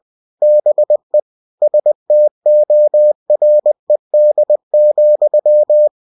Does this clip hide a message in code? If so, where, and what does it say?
Yes: Morse code from 0.4 to 5.9 s, heard as a keyed tone, about 20 wpm, 600 Hz. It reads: BE STORED,